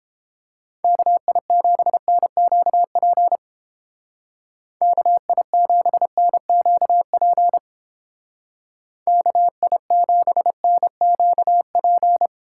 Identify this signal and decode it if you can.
Morse code, heard as a keyed tone, about 28 wpm, 700 Hz. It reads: KI7NQP KI7NQP KI7NQP